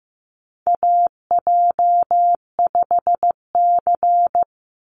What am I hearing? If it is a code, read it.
Morse code, heard as a keyed tone, about 15 wpm, 700 Hz. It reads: AJ5C
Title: NA SOTA Chaser Callsigns 15wpm